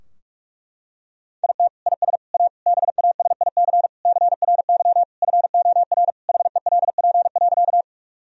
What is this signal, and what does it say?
Morse code, heard as a keyed tone, about 45 wpm, 700 Hz. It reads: IT IS A BASIC CRY FOR HELP.
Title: Sentences from Top 1000 Words - Review Part 3 - 45wpm